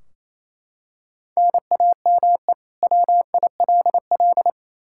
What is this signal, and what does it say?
Morse code, heard as a keyed tone, about 28 wpm, 700 Hz. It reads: NAME WILL
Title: K1USN SST QSO Practice 28wpm